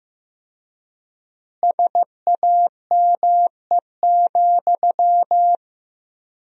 Morse code, heard as a keyed tone, about 16 wpm, 700 Hz. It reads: SAME,
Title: 4x Word-Spacing - Sentences from Top 300 Words - 15wpm